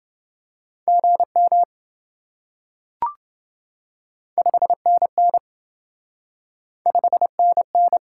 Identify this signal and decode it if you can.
Morse code, heard as a keyed tone, about 26 wpm, 700 Hz. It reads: GM 5NN 5NN